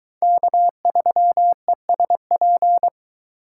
Morse code, heard as a keyed tone, about 22 wpm, 700 Hz. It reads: K3ESP